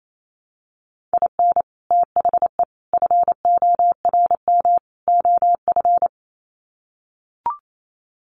Morse code, heard as a keyed tone, about 28 wpm, 700 Hz. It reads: IN THE FORM OF
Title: Sets of 4 Words 28wpm